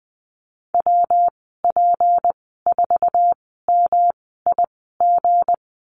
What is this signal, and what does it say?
Morse code, heard as a keyed tone, about 20 wpm, spaced as for 15 wpm, 700 Hz. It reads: WP4MIG
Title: 179 Lesson 29g - Slash - Call Signs 20wpm